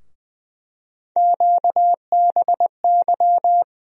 Morse code, heard as a keyed tone, about 21 wpm, 700 Hz. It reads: QBY
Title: USA Call Sign Postfixes - 3 Letters 20wpm